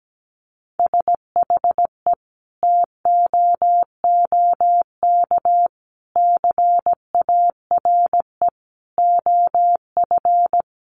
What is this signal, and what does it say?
Morse code, heard as a keyed tone, about 17 wpm, 700 Hz. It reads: SHE TOOK CARE OF
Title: Sentences from Top 1300 Words 17wpm